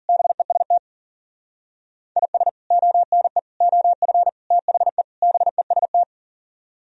Morse code, heard as a keyed tone, about 52 wpm, 700 Hz. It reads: BEST IS ONE OF THE BEST